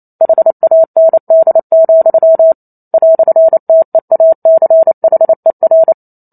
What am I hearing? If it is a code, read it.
Morse code, heard as a keyed tone, about 28 wpm, 650 Hz. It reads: HAND, "TEACHER
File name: Chapter 2 - 28 WPM